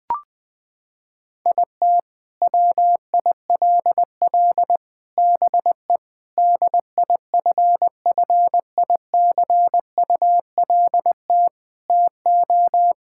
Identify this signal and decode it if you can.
Morse code, heard as a keyed tone, about 21 wpm, 700 Hz. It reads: IT WILL BE DIFFICULT TO
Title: Sentences from Top 1000 Words - Review Part 6 - 20wpm